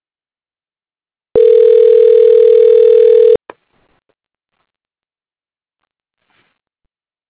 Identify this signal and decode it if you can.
Ringback tone (North America), 440 + 480 Hz.
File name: recordings / REec37e6a4a54f413c7b54bf910e45d9fd